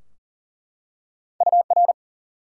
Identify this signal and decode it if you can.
Morse code, heard as a keyed tone, about 40 wpm, 700 Hz. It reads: UR
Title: Top 90 QSO Elements 40wpm